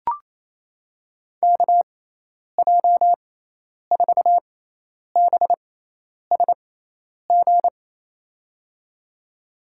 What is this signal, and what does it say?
Morse code, heard as a keyed tone, about 28 wpm, spaced as for 10 wpm, 700 Hz. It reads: KJ4BSG
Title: USA Call Signs - 6 Characters - 6x Character Spacing - 28wpm